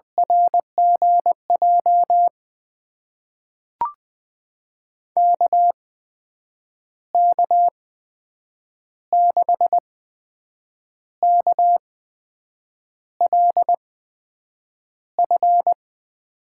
Morse code, heard as a keyed tone, about 20 wpm, spaced as for 7 wpm, 700 Hz. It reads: RGJ KK6KLF
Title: USA Call Signs - 6 Characters - 8x Character Spacing - 20wpm